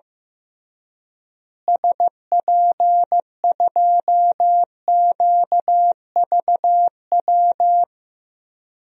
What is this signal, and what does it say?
Morse code, heard as a keyed tone, about 15 wpm, 700 Hz. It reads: SP2QVW
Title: DXCC-100 Call Signs - 6 Characters - Repeated 3x - 15wpm